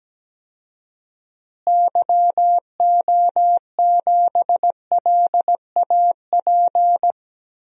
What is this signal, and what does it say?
Morse code, heard as a keyed tone, about 17 wpm, 700 Hz. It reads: YO7LAP
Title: DXCC-100 Call Signs - 6 Characters - Repeated 3x - 17wpm